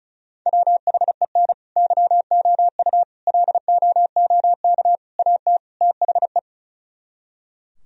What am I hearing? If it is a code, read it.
Morse code, heard as a keyed tone, about 43 wpm, 700 Hz. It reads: WHEN YOU LOOK AT THE
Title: Speed-Racing - Sets of 5 Words 35wpm